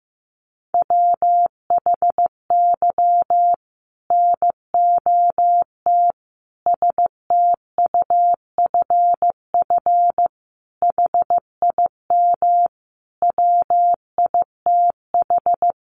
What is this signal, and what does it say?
Morse code, heard as a keyed tone, about 15 wpm, 700 Hz. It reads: WHY NOT STUFF HIM WITH
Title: Sentences from Top 1400 Words 15wpm